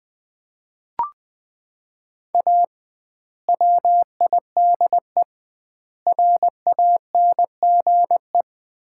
Morse code, heard as a keyed tone, about 20 wpm, 700 Hz. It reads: A WIDE RANGE
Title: 2x Word-Spacing - Sets of 4 Words - 20wpm